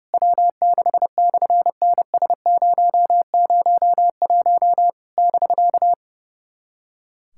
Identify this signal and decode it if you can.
Morse code, heard as a keyed tone, about 30 wpm, 700 Hz. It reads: W6/NS001 <BK>
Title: SOTA QSO Practice 30wpm